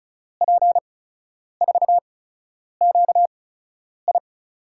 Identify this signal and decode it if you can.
Morse code, heard as a keyed tone, about 36 wpm, spaced as for 10 wpm, 700 Hz. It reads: P4QI
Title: USA Call Signs - 5 Characters - 8x Character Spacing - 35wpm